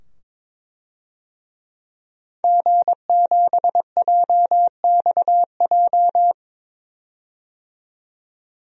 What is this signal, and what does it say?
Morse code, heard as a keyed tone, about 22 wpm, 700 Hz. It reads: G7JXJ
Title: DXCC-100 Call Signs - 5 Characters - 22wpm